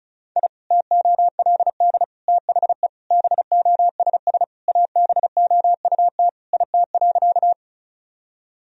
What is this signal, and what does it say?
Morse code, heard as a keyed tone, about 36 wpm, 700 Hz. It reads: I TOLD THE BOSS ABOUT IT.
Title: Sentences from Top 1000 Words - Review Part 4 - 35wpm